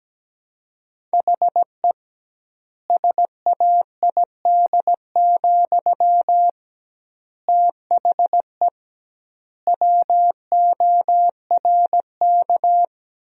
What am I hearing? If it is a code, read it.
Morse code, heard as a keyed tone, about 17 wpm, 700 Hz. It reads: HE SAID, THE WORK
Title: 2x Word-Spacing - Sentences from Top 400 Words - 17wpm